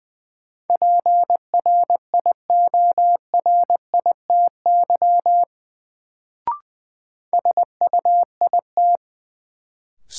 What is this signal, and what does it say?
Morse code, heard as a keyed tone, about 21 wpm, 700 Hz. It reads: PRIORITY SUIT
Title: Sentences from Top 1600 Words - New Word Drill - 20wpm